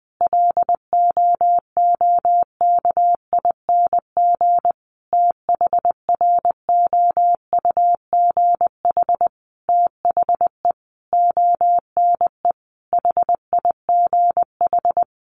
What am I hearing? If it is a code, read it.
Morse code, heard as a keyed tone, about 20 wpm, 700 Hz. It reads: LOOKING THROUGH THE ONE HIGH
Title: A Sound Of Thunder 20wpm